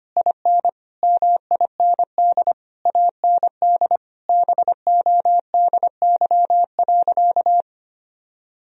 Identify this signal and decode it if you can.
Morse code, heard as a keyed tone, about 25 wpm, 700 Hz. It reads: IN MIND AND BODY.
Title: Sentences from Top 1000 Words - Review Part 2 - 25wpm